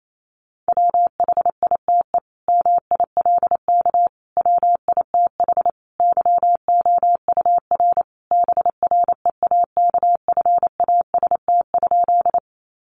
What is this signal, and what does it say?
Morse code, heard as a keyed tone, about 28 wpm, 700 Hz. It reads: WHITE MILK WITH YOUR BREAKFAST?
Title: Sentences from Top 1500 Words 28wpm